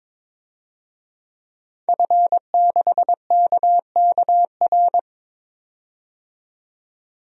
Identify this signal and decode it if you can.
Morse code, heard as a keyed tone, about 22 wpm, 700 Hz. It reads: F6KKR
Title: DXCC-100 Call Signs - 5 Characters - 22wpm